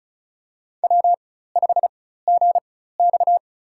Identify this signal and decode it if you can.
Morse code, heard as a keyed tone, about 36 wpm, spaced as for 16 wpm, 700 Hz. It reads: W5GX
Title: USA Call Signs - 4 Characters - 4x Character Spacing - 35wpm